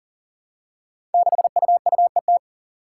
Morse code, heard as a keyed tone, about 38 wpm, 700 Hz. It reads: 6UUET
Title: Five Challenging Letter-Number Combinations 40wpm